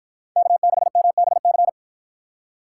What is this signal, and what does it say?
Morse code, heard as a keyed tone, about 52 wpm, 700 Hz. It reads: K6MBC